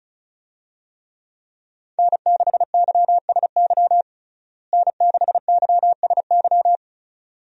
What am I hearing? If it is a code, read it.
Morse code, heard as a keyed tone, about 34 wpm, 700 Hz. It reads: N6YSY N6YSY
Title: USA Call Signs - 5 Characters - Repeated 2x - 35wpm